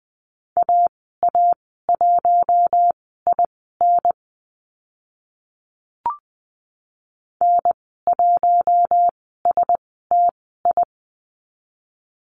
Morse code, heard as a keyed tone, about 20 wpm, spaced as for 15 wpm, 700 Hz. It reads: AA1IN N1STI
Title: 055 Lesson 10g - Letter r - Call Signs 20wpm